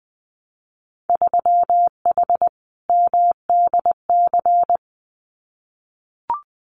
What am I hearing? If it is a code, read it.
Morse code, heard as a keyed tone, about 20 wpm, 700 Hz. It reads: 3H MDC
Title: Winter Field Day Exchanges Encore 20wpm